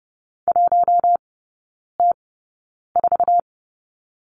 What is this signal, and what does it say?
Morse code, heard as a keyed tone, about 30 wpm, spaced as for 9 wpm, 700 Hz. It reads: 1T4
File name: course / 30:10wpm / 087 Lesson 15e - Number 5 - Three Characters 30wpm